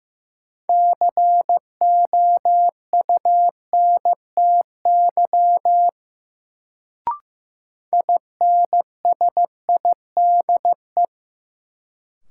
Morse code, heard as a keyed tone, about 15 wpm, 700 Hz. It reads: COUNTY INSIDE
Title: Sentences from Top 1200 Words - New Word Drill Alt - 15wpm